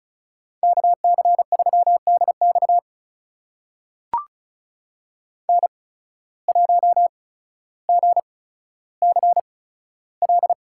Morse code, heard as a keyed tone, about 36 wpm, spaced as for 15 wpm, 700 Hz. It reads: KC3DX N1GCL
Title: USA Call Signs - 5 Characters - 8x Character Spacing - 35wpm